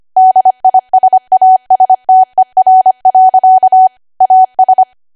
Morse code, heard as a keyed tone, about 25 wpm, 750 Hz. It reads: DISASTER. AS
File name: Archive / 25 WPM / 25WPM